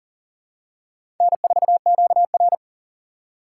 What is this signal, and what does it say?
Morse code, heard as a keyed tone, about 42 wpm, 700 Hz. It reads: N4QR
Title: USA Call Signs - 4 Characters - 2x Character Spacing - 40wpm